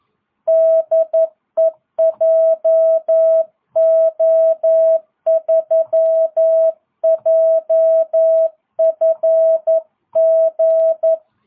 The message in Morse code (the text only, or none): DE JO3JFG